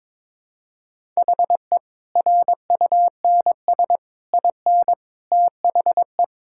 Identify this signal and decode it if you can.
Morse code, heard as a keyed tone, about 22 wpm, 700 Hz. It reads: HE RUNS IN THE
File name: practice / 22wpm / Easy Sentences with Hints - 22wpm